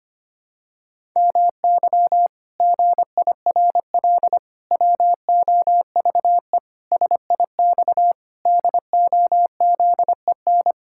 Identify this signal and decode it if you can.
Morse code, heard as a keyed tone, about 25 wpm, 700 Hz. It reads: MY GIRL WOVE SIX DOZEN
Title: Pangram Challenge v2 - 25wpm